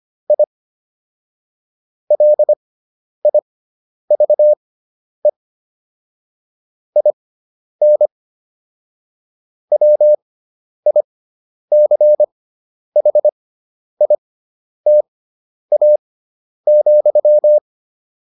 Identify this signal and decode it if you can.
Morse code, heard as a keyed tone, about 25 wpm, spaced as for 10 wpm, 600 Hz. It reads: I LIVE IN WICHITA,
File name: QSO309 10